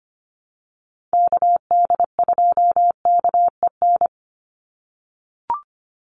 Morse code, heard as a keyed tone, about 25 wpm, 700 Hz. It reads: KD2KEN
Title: USA Call Signs 25wpm